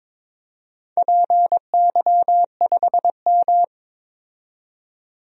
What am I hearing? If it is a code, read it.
Morse code, heard as a keyed tone, about 22 wpm, 700 Hz. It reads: PY5M
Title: DXCC-100 Call Signs - 4 Characters - 22wpm